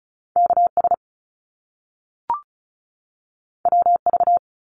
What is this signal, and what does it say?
Morse code, heard as a keyed tone, about 36 wpm, 700 Hz. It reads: KS WV
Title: US States 35wpm